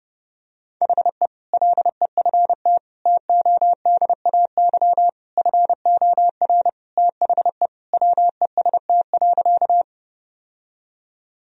Morse code, heard as a keyed tone, about 32 wpm, 700 Hz. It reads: HE LEFT TODAY FOR THE WEST.
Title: Sentences from Top 1000 Words - Review Part 2 - 30wpm